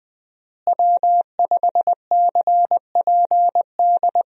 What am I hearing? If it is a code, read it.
Morse code, heard as a keyed tone, about 21 wpm, 700 Hz. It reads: W5CPD
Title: USA Call Signs - 5 Characters - Repeated 3x - 20wpm